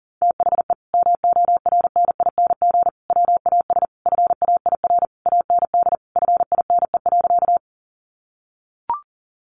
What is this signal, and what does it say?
Morse code, heard as a keyed tone, about 40 wpm, 700 Hz. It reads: THE MORNING WAS FAIR AND FINE.
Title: Sentences from Top 1300 Words 40wpm